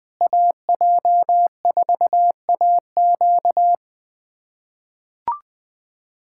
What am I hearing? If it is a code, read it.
Morse code, heard as a keyed tone, about 20 wpm, 700 Hz. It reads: AJ4AQ
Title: USA Callsign DE Callsign 20wpm